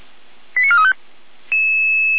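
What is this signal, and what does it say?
Five-tone selective call (ZVEI), heard as quick runs of five tones, tones of about 80 ms. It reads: 89435
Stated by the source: ZVEI five-tone selective call received with an SDR.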